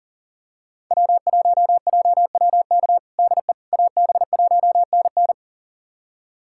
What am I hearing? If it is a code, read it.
Morse code, heard as a keyed tone, about 40 wpm, 700 Hz. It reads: W1JWK DE AB1NN
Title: USA Callsign DE Callsign - 5 Characters - 40wpm